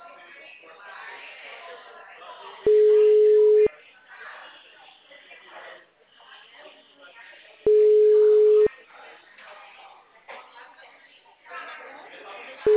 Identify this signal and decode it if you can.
Ringback tone (Europe, 425 Hz), with speech.